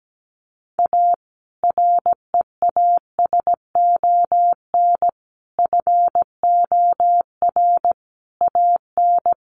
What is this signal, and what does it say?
Morse code, heard as a keyed tone, about 17 wpm, 700 Hz. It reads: A REASON FOR AN
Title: Sentences from Top 1300 Words 17wpm